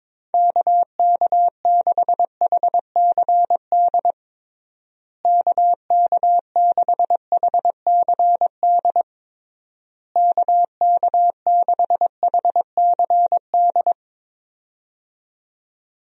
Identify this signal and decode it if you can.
Morse code, heard as a keyed tone, about 22 wpm, 700 Hz. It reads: KK6HCD KK6HCD KK6HCD
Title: USA Call Signs - 6 Characters - Repeated 3x - 22wpm